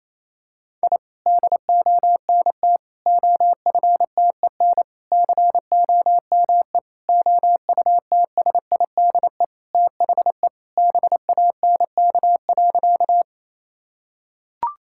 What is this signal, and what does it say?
Morse code, heard as a keyed tone, about 27 wpm, 700 Hz. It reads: I DONT OFTEN COME OUTSIDE THE BANK.
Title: Sentences from Top 1000 Words - Review Part 6 - 28wpm